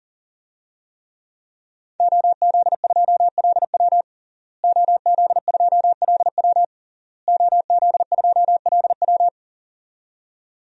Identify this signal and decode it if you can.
Morse code, heard as a keyed tone, about 37 wpm, 700 Hz. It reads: OZ2LW OZ2LW OZ2LW
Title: DXCC-100 Call Signs - 5 Characters - Repeated 3x - 40wpm